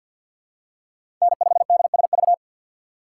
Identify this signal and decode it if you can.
Morse code, heard as a keyed tone, about 53 wpm, 700 Hz. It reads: N5DSV
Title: USA Call Signs - 5 Characters - 6x Character Spacing - 50wpm